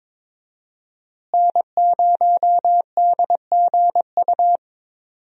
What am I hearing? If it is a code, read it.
Morse code, heard as a keyed tone, about 22 wpm, 700 Hz. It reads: N0DGU
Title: USA Call Signs - 5 Characters - Repeated 3x - 22wpm